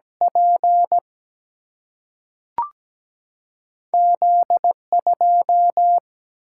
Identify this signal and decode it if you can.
Morse code, heard as a keyed tone, about 17 wpm, 700 Hz. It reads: P Z 2
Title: Single-Letter Single-Number 17wpm